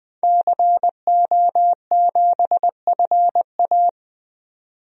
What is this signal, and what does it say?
Morse code, heard as a keyed tone, about 20 wpm, 700 Hz. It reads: CO7FA
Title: DXCC-100 Call Signs - 5 Characters - 20wpm